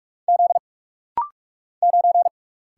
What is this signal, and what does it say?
Morse code, heard as a keyed tone, about 45 wpm, 700 Hz. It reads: Z 9
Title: Single Letter-Number Alt - Rapid-Fire 45wpm